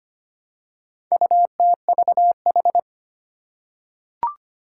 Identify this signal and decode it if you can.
Morse code, heard as a keyed tone, about 25 wpm, 700 Hz. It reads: UTVH
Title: Four Challenging Letter-Number Combinations 25wpm